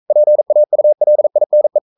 Morse code, heard as a keyed tone, about 42 wpm, 600 Hz. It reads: PAULINE
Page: Romanen Never af Ken Follet på engelsk.